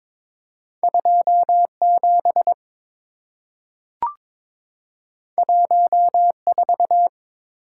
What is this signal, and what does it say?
Morse code, heard as a keyed tone, about 22 wpm, 690 Hz. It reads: 27 14
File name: practice / 22wpm / Two Letter-Number Combinations - 22wpm